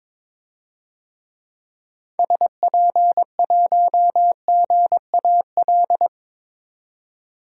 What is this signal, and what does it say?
Morse code, heard as a keyed tone, about 21 wpm, 700 Hz. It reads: SP1GAL
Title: DXCC-100 Call Signs - 6 Characters - 22wpm